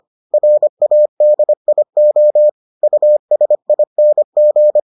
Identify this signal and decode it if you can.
Morse code, heard as a keyed tone, about 25 wpm, 600 Hz. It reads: RADIO USING